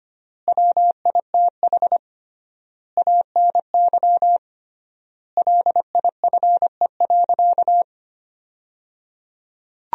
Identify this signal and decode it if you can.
Morse code, heard as a keyed tone, about 25 wpm, 700 Hz. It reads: WITH ANY LIFE.
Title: 3x Word-Spacing - Sentences from Top 300 Words - 25wpm